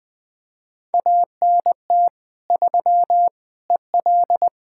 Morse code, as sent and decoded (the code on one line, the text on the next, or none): .- -. - / ...-- / . .-..
ANT 3 EL